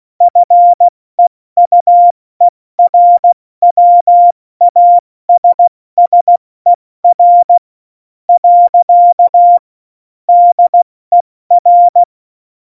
Morse code, heard as a keyed tone, about 16 wpm, spaced as for 13 wpm, 700 Hz. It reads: FEUERWASSER. DER
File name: Morsekurs / Witze / Witz 04. September 2018, 70 F80